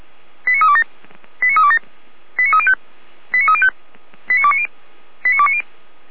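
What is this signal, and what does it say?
Five-tone selective call (ZVEI), heard as quick runs of five tones, tones of about 70 ms. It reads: 89327 89327 89305 89305 89290 89290
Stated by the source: Received by SDR.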